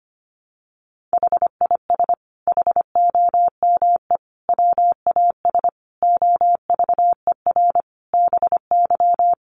Morse code, heard as a keyed tone, about 25 wpm, 700 Hz. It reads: HIS HOME WAS OVER BY
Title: Sentences from Top 1500 Words 25wpm